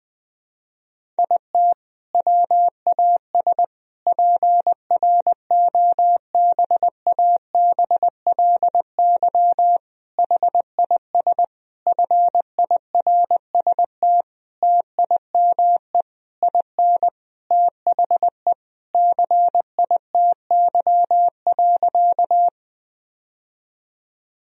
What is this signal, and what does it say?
Morse code, heard as a keyed tone, about 21 wpm, 700 Hz. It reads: IT WAS PROBABLY HIS FIRST TIME IN THE CITY.
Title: Sentences from Top 1000 Words - Review Part 4 - 20wpm